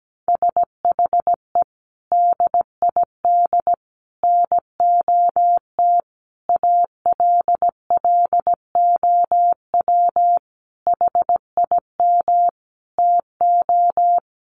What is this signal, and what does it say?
Morse code, heard as a keyed tone, about 17 wpm, 700 Hz. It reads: SHE DID NOT ALLOW HIM TO
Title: Sentences from Top 1200 Words 17wpm